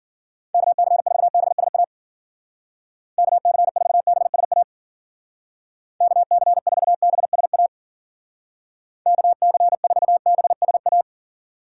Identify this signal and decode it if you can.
Morse code, heard as a keyed tone, about 50 wpm, 700 Hz. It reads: KC4BSA KC4BSA KC4BSA KC4BSA
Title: Speed Racing - USA Call Signs 40wpm